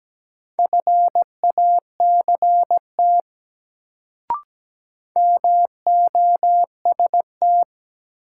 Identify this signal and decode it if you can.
Morse code, heard as a keyed tone, about 17 wpm, 700 Hz. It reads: FACT MOST
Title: Sets of 2 Words - New Word Drill - 17wpm